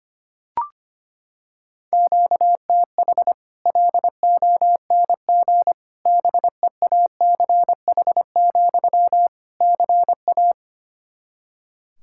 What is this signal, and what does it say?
Morse code, heard as a keyed tone, about 25 wpm, 700 Hz. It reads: QTH LONG BEACH, CA
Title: Practice QSO - Exchange 1 - Beginning - 25wpm